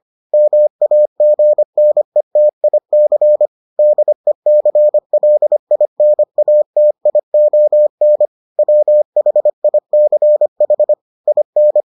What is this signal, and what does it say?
Morse code, heard as a keyed tone, about 25 wpm, 600 Hz. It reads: MAGNETIC DECLINATION WHICH IN